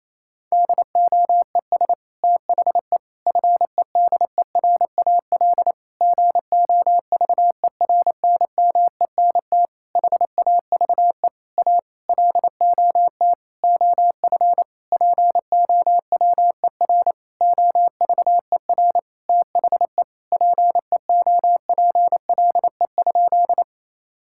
Morse code, heard as a keyed tone, about 28 wpm, 700 Hz. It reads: DOES THE FEDERAL GOVERNMENT HAVE A LOT OF POWER OVER THE PEOPLE?
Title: Sentences from Top 1000 Words - Review Part 5 - 28wpm